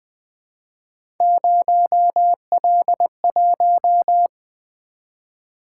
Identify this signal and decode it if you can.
Morse code, heard as a keyed tone, about 20 wpm, 700 Hz. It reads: 0L1
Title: Three Letter-Number Combinations - 20wpm